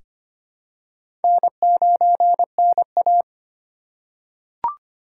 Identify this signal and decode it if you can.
Morse code, heard as a keyed tone, about 25 wpm, 700 Hz. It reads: N9NA
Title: USA Call Signs - 4 Characters - 6x Character Spacing - 25wpm